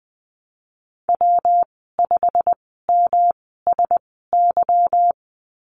Morse code, heard as a keyed tone, about 20 wpm, spaced as for 15 wpm, 700 Hz. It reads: W5MSY
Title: 173 Lesson 28g - Number 9 - Call Signs 20wpm